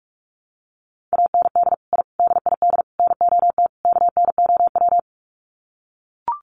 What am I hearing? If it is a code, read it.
Morse code, heard as a keyed tone, about 44 wpm, 700 Hz. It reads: AND I DID NOT KNOW
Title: Sets of 5 Words Encore 45wpm